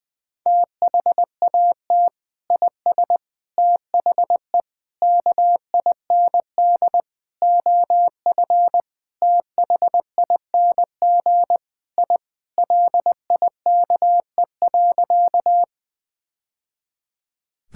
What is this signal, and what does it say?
Morse code, heard as a keyed tone, about 20 wpm, 700 Hz. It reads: THAT IS THE KIND OF THING I LIKE.
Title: Sentences with 3-Word Phrases - Spoken Beginning - 20wpm